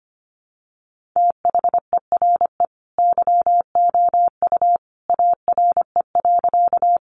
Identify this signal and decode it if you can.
Morse code, heard as a keyed tone, about 25 wpm, 700 Hz. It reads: THERE YOU ARE.
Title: Sentences from Top 100 Words 25wpm